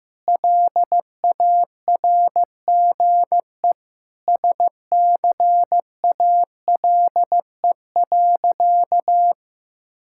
Morse code, heard as a keyed tone, about 15 wpm, 690 Hz. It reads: LARGE SCALE.
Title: Sentences from Top 1000 Words - Review Part 5 - 15wpm